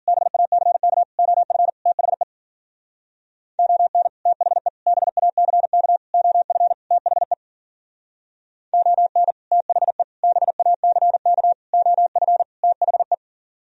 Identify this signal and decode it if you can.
Morse code, heard as a keyed tone, about 50 wpm, 700 Hz. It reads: BACK OF THE ON THE BACK OF THE ON THE BACK OF THE
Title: Speed-Racing - Sets of 5 Words 40wpm